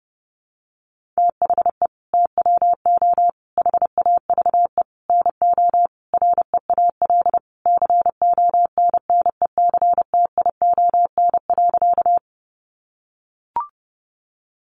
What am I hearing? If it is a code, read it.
Morse code, heard as a keyed tone, about 30 wpm, 700 Hz. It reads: THE TWO HAVE NO REAL CONNECTION.
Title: Sentences from Top 1300 Words 30wpm